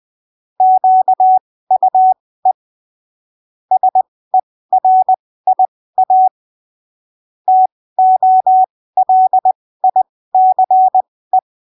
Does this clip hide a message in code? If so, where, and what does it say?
Yes: Morse code from 0.6 to 11.4 s, heard as a keyed tone, about 20 wpm, spaced as for 15 wpm, 750 Hz. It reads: QUE SERIA TOLICE